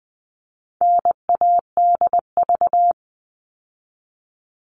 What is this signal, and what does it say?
Morse code, heard as a keyed tone, about 20 wpm, 700 Hz. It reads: NADV
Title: All Cut Numbers - 4-Digit 20wpm